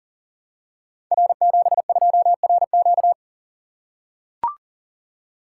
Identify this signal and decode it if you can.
Morse code, heard as a keyed tone, about 40 wpm, 700 Hz. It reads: R72RQ
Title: Code Groups - 5 Letter:Numbers - 40wpm